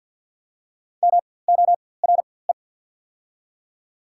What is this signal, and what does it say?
Morse code, heard as a keyed tone, about 48 wpm, spaced as for 23 wpm, 700 Hz. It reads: MORE
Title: 4x Character Spacing - Top 300 Words - 50wpm